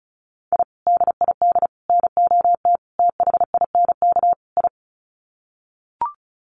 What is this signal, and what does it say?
Morse code, heard as a keyed tone, about 34 wpm, 700 Hz. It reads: I DID NOT THINK I
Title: Sets of 5 Words Encore 35wpm